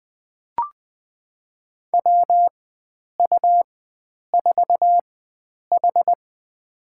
Morse code, heard as a keyed tone, about 20 wpm, spaced as for 9 wpm, 700 Hz. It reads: WU4H
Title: USA Call Signs - 4 Characters - 4x Character Spacing - 20wpm